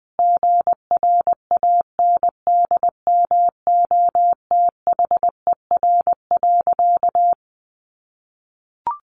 Morse code, heard as a keyed tone, about 20 wpm, 700 Hz. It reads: GRANDMOTHER.
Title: Sentences from Top 1200 Words 20wpm